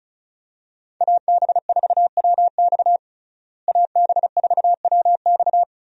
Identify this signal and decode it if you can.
Morse code, heard as a keyed tone, about 36 wpm, 700 Hz. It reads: AB4WX AB4WX
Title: USA Call Signs - 5 Characters - Repeated 3x - 35wpm